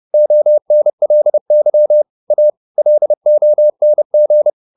Morse code, heard as a keyed tone, about 30 wpm, 600 Hz. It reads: ONLY A LONG
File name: the old man and the sea / the old man and the sea 1 30wpm